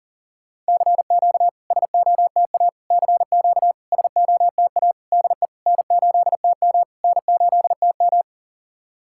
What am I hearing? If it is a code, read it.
Morse code, heard as a keyed tone, about 40 wpm, 690 Hz. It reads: CQ SOTA CQ SOTA DE N8TM N8TM
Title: SOTA QSO Practice 40wpm